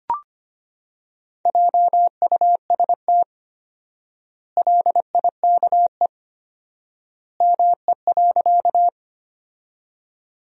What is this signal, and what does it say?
Morse code, heard as a keyed tone, about 25 wpm, 700 Hz. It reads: JUST LIKE ME.
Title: 4x Word-Spacing - Sentences from Top 200 Words - 25wpm